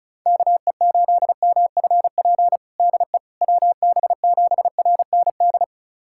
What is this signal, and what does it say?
Morse code, heard as a keyed tone, about 35 wpm, 700 Hz. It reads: KE8MFP DE WB7RND
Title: USA Callsign DE Callsign - 6 Characters - 35wpm